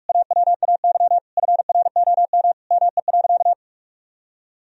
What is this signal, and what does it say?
Morse code, heard as a keyed tone, about 42 wpm, 700 Hz. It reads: AWAY FROM ME.
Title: Sentences from Top 1000 Words - Review Part 2 - 45wpm